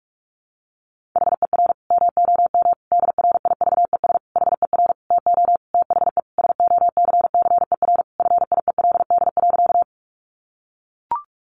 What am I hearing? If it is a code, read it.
Morse code, heard as a keyed tone, about 44 wpm, 700 Hz. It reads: HER MOM DRIVES HER TO THE SOCCER FIELD.
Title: Easy Phrases 45wpm